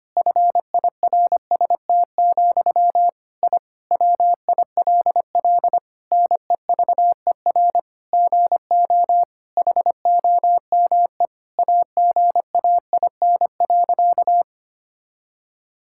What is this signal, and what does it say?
Morse code, heard as a keyed tone, about 25 wpm, 700 Hz. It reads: FIRST, I WILL NEVER GO HOME AGAIN.
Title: Sentences from Top 1000 Words - Review Part 6 - 25wpm